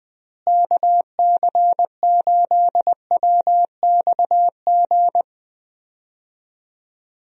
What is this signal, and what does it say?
Morse code, heard as a keyed tone, about 21 wpm, 700 Hz. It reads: KC8WXG
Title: USA Call Signs - 6 Characters - Repeated 2x - 20wpm